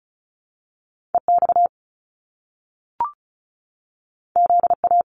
Morse code, heard as a keyed tone, about 36 wpm, 700 Hz. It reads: EX ZA
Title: Top Two Letter Combinations in Words 35wpm